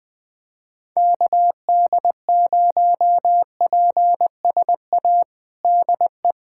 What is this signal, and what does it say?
Morse code, heard as a keyed tone, about 20 wpm, 700 Hz. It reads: KD0PSA DE
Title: USA Callsign DE Callsign - 6 Characters - 20wpm